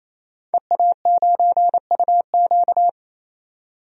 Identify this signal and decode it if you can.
Morse code, heard as a keyed tone, about 28 wpm, 700 Hz. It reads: EA9UQ
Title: DXCC-100 Call Signs - 5 Characters - 28wpm